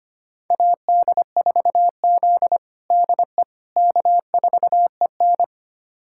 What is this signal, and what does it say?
Morse code, heard as a keyed tone, about 25 wpm, 700 Hz. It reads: AD4Z DE K4EN
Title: USA Callsign DE Callsign - 4 Characters - 25wpm